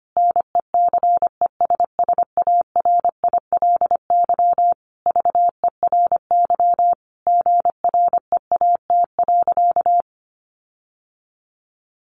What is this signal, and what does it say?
Morse code, heard as a keyed tone, about 25 wpm, 700 Hz. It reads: NECESSARILY VERY GREAT.
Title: Sentences from Top 1400 Words 25wpm